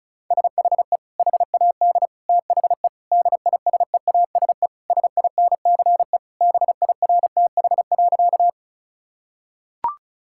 Morse code, heard as a keyed tone, about 36 wpm, 700 Hz. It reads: SHE HAD THE DISEASE SINCE BIRTH.
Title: Sentences from Top 1000 Words - Review Part 6 - 35wpm